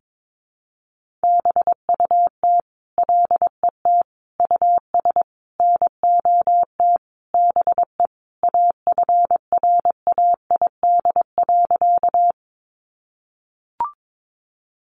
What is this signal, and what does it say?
Morse code, heard as a keyed tone, about 22 wpm, 700 Hz. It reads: BUT LET US NOT BE AFRAID.
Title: Sentences from Top 1500 Words 22wpm